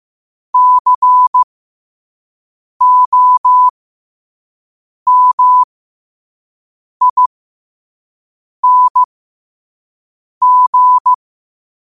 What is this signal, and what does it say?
Morse code, heard as a keyed tone, about 15 wpm, spaced as for 5 wpm, 1000 Hz. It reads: COMING
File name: qotdfiles / 5wpm